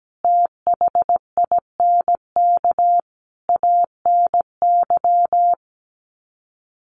Morse code, heard as a keyed tone, about 17 wpm, 700 Hz. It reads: THINK ANY
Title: Sets of 5 Words Encore 17wpm